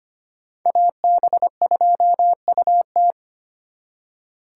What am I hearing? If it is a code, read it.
Morse code, heard as a keyed tone, about 33 wpm, 700 Hz. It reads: AB2UT